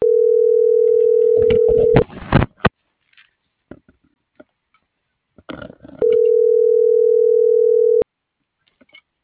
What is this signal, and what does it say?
Ringback tone (North America), 440 + 480 Hz.